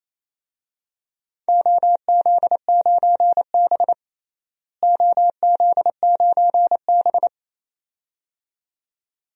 Morse code, heard as a keyed tone, about 28 wpm, 700 Hz. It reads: OZ9B OZ9B
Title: DXCC-100 Call Signs - 4 Characters - Repeated 2x - 28wpm